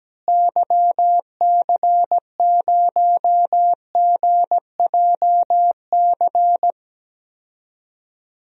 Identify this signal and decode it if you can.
Morse code, heard as a keyed tone, about 17 wpm, 700 Hz. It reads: YC0GJC
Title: DXCC-100 Call Signs - 6 Characters - Repeated 3x - 17wpm